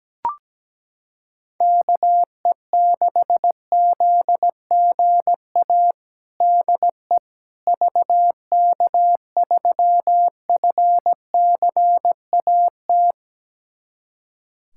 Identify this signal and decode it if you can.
Morse code, heard as a keyed tone, about 17 wpm, 700 Hz. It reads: KE6ZGA DE VK3FCAT
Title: USA Callsign DE Callsign 17wpm